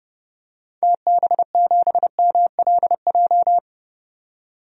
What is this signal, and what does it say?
Morse code, heard as a keyed tone, about 29 wpm, 700 Hz. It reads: TB7MLJ
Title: DXCC-100 Call Signs - 6 Characters - 30wpm